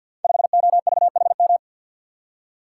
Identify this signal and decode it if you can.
Morse code, heard as a keyed tone, about 47 wpm, 700 Hz. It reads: 5OVHM
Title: Code Groups - 5 Letter:Numbers - 50wpm